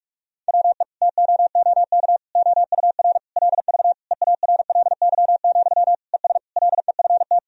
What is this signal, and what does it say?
Morse code, heard as a keyed tone, about 46 wpm, 700 Hz. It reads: WE TOOK OUR LV EARLY, ES LEFT
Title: Sentences from Top 400 Words and Abbreviations 45wpm